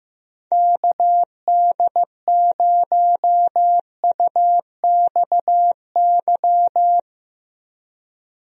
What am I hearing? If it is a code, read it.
Morse code, heard as a keyed tone, about 15 wpm, 700 Hz. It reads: KD0UXY
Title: USA Call Signs - 6 Characters - Repeated 3x - 15wpm